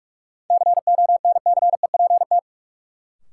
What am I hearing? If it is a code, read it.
Morse code, heard as a keyed tone, about 46 wpm, 700 Hz. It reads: CONCEPT
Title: Sentences from Top 1100 Words - New Word Drill - 45wpm